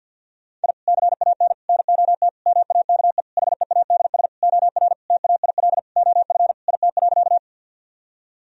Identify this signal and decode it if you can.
Morse code, heard as a keyed tone, about 50 wpm, 700 Hz. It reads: I CAN NOT MAKE HEADS OR TAIL OF IT.
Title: Sentences from Top 1000 Words - Review Part 3 - 50wpm